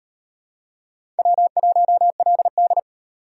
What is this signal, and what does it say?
Morse code, heard as a keyed tone, about 38 wpm, 700 Hz. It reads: W1LD